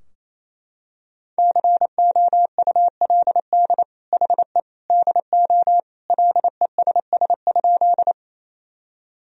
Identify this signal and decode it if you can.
Morse code, heard as a keyed tone, about 27 wpm, 700 Hz. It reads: COULD HE DO LESS?
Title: Sentences from Top 1000 Words - Review Part 2 - 28wpm